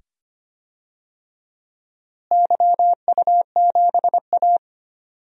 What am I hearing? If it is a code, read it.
Morse code, heard as a keyed tone, about 25 wpm, 700 Hz. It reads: YU7A
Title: DXCC-100 Call Signs - 4 Characters - Repeated 2x - 25wpm